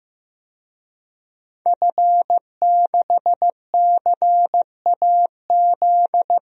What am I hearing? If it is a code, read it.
Morse code, heard as a keyed tone, about 15 wpm, 700 Hz. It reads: F6CAZ
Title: DXCC-100 Call Signs - 5 Characters - Repeated 3x - 15wpm